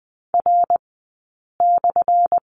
Morse code, heard as a keyed tone, about 20 wpm, 700 Hz. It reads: R /
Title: 190 Lesson 31d - Letter v - Two Characters 20wpm